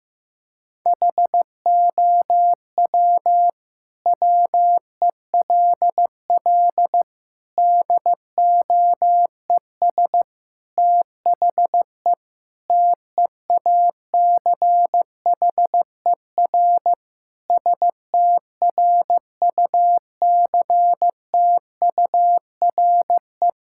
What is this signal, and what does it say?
Morse code, heard as a keyed tone, about 15 wpm, 700 Hz. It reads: HOW WELL DOES THE TEACHER STRUCTURE
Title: Sentences from Top 1000 Words - Review Part 6 - 15wpm